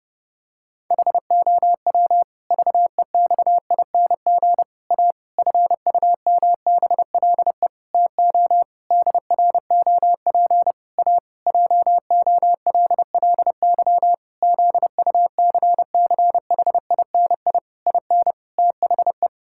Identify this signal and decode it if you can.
Morse code, heard as a keyed tone, about 30 wpm, 700 Hz. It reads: HOW VEXING A FUMBLE TO DROP A JOLLY ZUCCHINI IN THE
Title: Pangram Challenge v2 - 30wpm